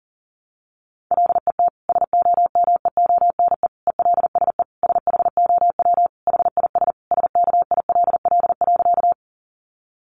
Morse code, heard as a keyed tone, about 37 wpm, 700 Hz. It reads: LET SOMEONE ELSE SHOW HIS SKILL.
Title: Sentences from Top 1200 Words 40wpm